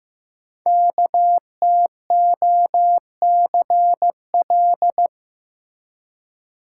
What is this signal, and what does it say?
Morse code, heard as a keyed tone, about 15 wpm, 700 Hz. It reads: KTOCL
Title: Code Groups - 5 Letters - 15wpm